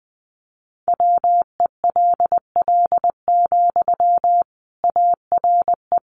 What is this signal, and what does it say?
Morse code, heard as a keyed tone, about 20 wpm, 700 Hz. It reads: WELL, ARE
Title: Sentences from Top 400 Words 20wpm